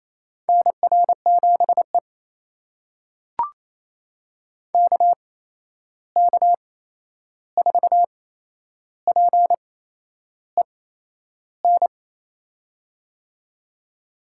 Morse code, heard as a keyed tone, about 28 wpm, spaced as for 11 wpm, 700 Hz. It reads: NR7E KK4PEN
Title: USA Call Signs - 8x Character Spacing - 28wpm